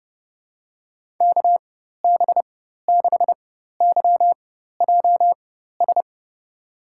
Morse code, heard as a keyed tone, about 30 wpm, spaced as for 14 wpm, 700 Hz. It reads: KB6YJS
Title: USA Call Signs - 6 Characters - 4x Character Spacing - 30wpm